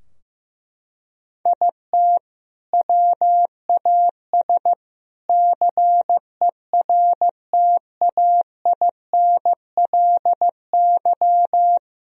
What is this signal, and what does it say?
Morse code, heard as a keyed tone, about 15 wpm, 700 Hz. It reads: IT WAS CERTAINLY
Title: Sentences from Top 1000 Words - Review Part 3 - 15wpm